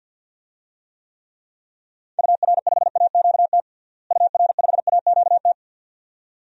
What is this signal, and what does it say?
Morse code, heard as a keyed tone, about 53 wpm, 700 Hz. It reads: UR5AQT UR5AQT
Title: DXCC-100 Call Signs - 6 Characters - Repeated 2x - 50wpm